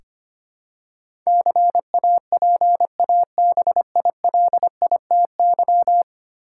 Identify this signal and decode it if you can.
Morse code, heard as a keyed tone, about 25 wpm, 700 Hz. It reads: CAPABILITY
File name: Ten Letter Words - 25wpm